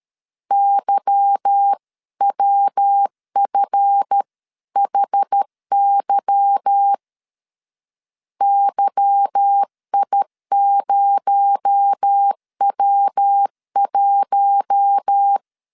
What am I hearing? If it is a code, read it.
Morse code, heard as a keyed tone, about 13 wpm, 800 Hz. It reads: YWFHY YI0W1
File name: LEZIONI / 8 YJ45 / 8 Y 12wpm 2228